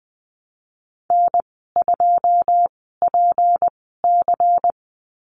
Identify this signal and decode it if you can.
Morse code, heard as a keyed tone, about 20 wpm, spaced as for 15 wpm, 700 Hz. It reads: N2PC
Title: 193 Lesson 31g - Letter v - Call Signs 20wpm